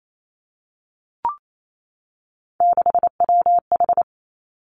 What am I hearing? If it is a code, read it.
Morse code, heard as a keyed tone, about 28 wpm, 700 Hz. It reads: 6WH
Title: Three Challenging Letter-Number Combinations 28wpm